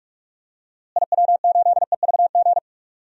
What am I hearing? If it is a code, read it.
Morse code, heard as a keyed tone, about 48 wpm, 700 Hz. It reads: IW8EVG
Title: DXCC-100 Call Signs - 6 Characters - Repeated 2x - 45wpm